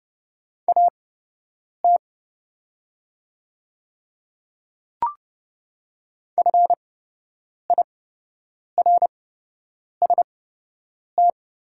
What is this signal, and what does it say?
Morse code, heard as a keyed tone, about 30 wpm, spaced as for 8 wpm, 700 Hz. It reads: AT FIRST
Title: 8x Character Spacing - Top 100 Words - 30wpm